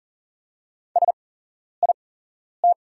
Morse code, heard as a keyed tone, about 40 wpm, spaced as for 11 wpm, 700 Hz. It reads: SIT
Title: 8x Character Spacing - Top 500 Words - 40wpm